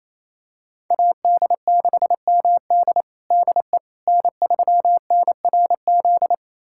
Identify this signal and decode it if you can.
Morse code, heard as a keyed tone, about 27 wpm, 700 Hz. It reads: AD6MD DE N3NRZ
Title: USA Callsign DE Callsign - 5 Characters - 28wpm